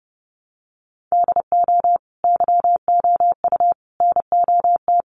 Morse code, heard as a keyed tone, about 30 wpm, 700 Hz. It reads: DO YOU NOT
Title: Sentences from Top 600 Words 30wpm